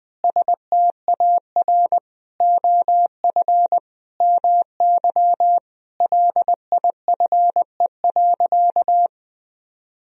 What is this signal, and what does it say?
Morse code, heard as a keyed tone, about 20 wpm, 700 Hz. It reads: STAR OF MY LIFE.
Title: Sentences from Top 1000 Words - Review Part 2 - 20wpm